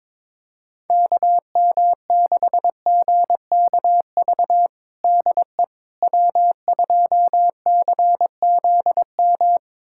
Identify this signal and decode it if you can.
Morse code, heard as a keyed tone, about 22 wpm, 700 Hz. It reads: KM6GKV DE W2CZM
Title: USA Callsign DE Callsign 22wpm